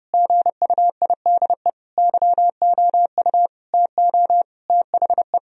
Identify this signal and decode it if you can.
Morse code, heard as a keyed tone, about 32 wpm, 700 Hz. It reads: GUIDE YOU TO THE
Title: Sentences from Top 1000 Words - Review Part 3 - 30wpm